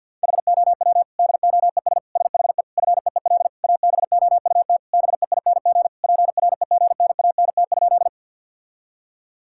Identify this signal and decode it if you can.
Morse code, heard as a keyed tone, about 50 wpm, 700 Hz. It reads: HOW DOES SHE FEEL ABOUT BEING PREGNANT?
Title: Sentences from Top 1000 Words - Review Part 6 - 50wpm